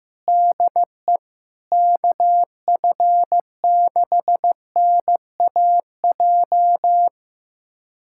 Morse code, heard as a keyed tone, about 15 wpm, 700 Hz. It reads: DE KF6NAJ
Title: USA Callsign DE Callsign - 6 Characters - 15wpm